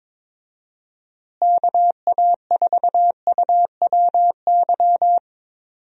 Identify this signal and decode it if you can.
Morse code, heard as a keyed tone, about 22 wpm, 700 Hz. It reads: KA4UWY
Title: USA Call Signs - 6 Characters - Repeated 3x - 22wpm